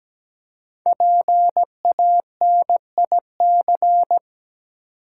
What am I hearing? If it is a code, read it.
Morse code, heard as a keyed tone, about 17 wpm, 700 Hz. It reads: PANIC
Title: Five Letter Words - 17wpm